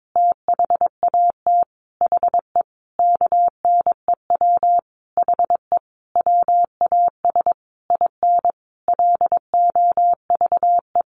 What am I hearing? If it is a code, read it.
Morse code, heard as a keyed tone, about 22 wpm, 700 Hz. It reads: THAT HE KNEW HE WAS IN LOVE
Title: Sentences from Top 1300 Words 22wpm